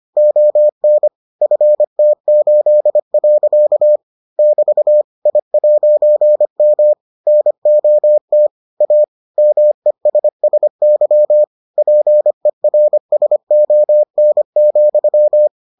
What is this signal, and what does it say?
Morse code, heard as a keyed tone, about 25 wpm, 600 Hz. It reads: ON FT8. <BT> I'M NOT A MESSY PERSON,